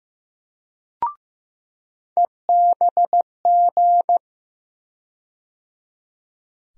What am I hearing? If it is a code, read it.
Morse code, heard as a keyed tone, about 15 wpm, 700 Hz. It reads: EBG
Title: USA Call Sign Postfixes - 3 Letters 15wpm